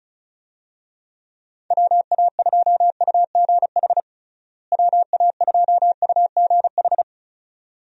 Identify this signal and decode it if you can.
Morse code, heard as a keyed tone, about 34 wpm, 700 Hz. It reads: WA2UGH WA2UGH
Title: USA Call Signs - 6 Characters - Repeated 2x - 35wpm